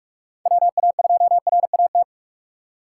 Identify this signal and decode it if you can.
Morse code, heard as a keyed tone, about 44 wpm, 700 Hz. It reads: WA2RAT
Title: USA Call Signs - 6 Characters - 6x Character Spacing - 45wpm